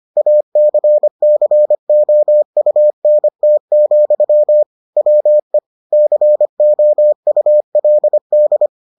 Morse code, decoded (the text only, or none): ACCOUNT, WE COULD